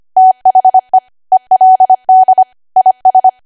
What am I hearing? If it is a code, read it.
Morse code, heard as a keyed tone, about 25 wpm, 750 Hz. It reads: THE ELD IS
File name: Archive / 25 WPM / 25WPM